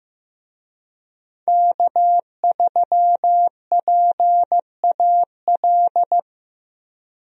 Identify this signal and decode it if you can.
Morse code, heard as a keyed tone, about 16 wpm, 700 Hz. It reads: K3PAL
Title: USA Call Signs - 5 Characters - Repeated 2x - 15wpm